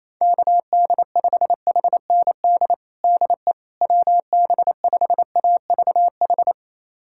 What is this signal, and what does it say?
Morse code, heard as a keyed tone, about 28 wpm, 700 Hz. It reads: KD5HND DE WB5AVH
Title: USA Callsign DE Callsign - 6 Characters - 28wpm